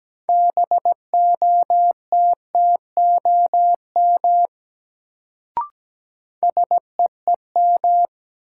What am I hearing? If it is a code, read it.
Morse code, heard as a keyed tone, about 17 wpm, 700 Hz. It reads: BOTTOM SEEM
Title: Sets of 3 Words - New Word Drill - 17wpm